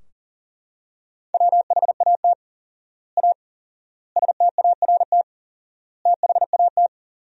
Morse code, heard as a keyed tone, about 40 wpm, 700 Hz. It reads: WHAT A START THAT
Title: 4x Word-Spacing - Sentences from Top 300 Words - 40wpm